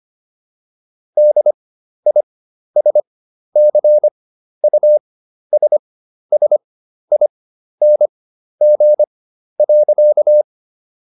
Morse code, heard as a keyed tone, about 25 wpm, spaced as for 12 wpm, 600 Hz. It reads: DISCUSSING.